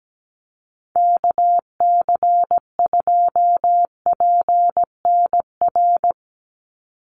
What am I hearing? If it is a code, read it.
Morse code, heard as a keyed tone, about 17 wpm, 700 Hz. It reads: KC2PNR
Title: USA Call Signs 17wpm